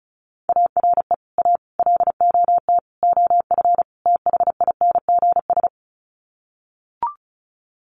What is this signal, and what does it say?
Morse code, heard as a keyed tone, about 34 wpm, 700 Hz. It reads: ARE A LOT OF THINGS
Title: Sets of 5 Words Encore2 35wpm